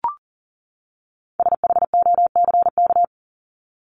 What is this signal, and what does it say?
Morse code, heard as a keyed tone, about 40 wpm, 700 Hz. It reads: SHOCK
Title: Top 1751-2000 Words 40wpm